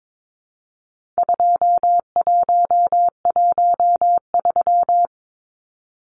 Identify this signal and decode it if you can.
Morse code, heard as a keyed tone, about 22 wpm, 700 Hz. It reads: 2113
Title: Numbers - 4-Digit 22wpm